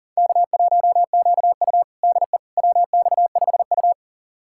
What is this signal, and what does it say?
Morse code, heard as a keyed tone, about 40 wpm, 700 Hz. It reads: K1QU DE WX5U
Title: USA Callsign DE Callsign - 4 Characters - 40wpm